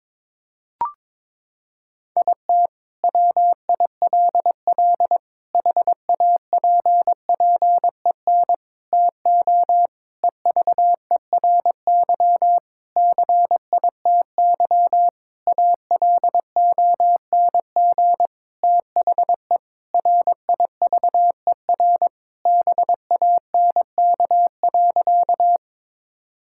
Morse code, heard as a keyed tone, about 22 wpm, 700 Hz. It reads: IT WILL HAPPEN TO EVERY CITY ALONG THE RIVER BANK.
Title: Sentences from Top 1000 Words - Review Part 6 - 22wpm